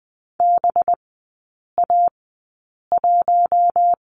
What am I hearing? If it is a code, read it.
Morse code, heard as a keyed tone, about 20 wpm, 700 Hz. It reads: B A 1
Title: 218 Lesson 35e - Number 0 - Three Characters 20wpm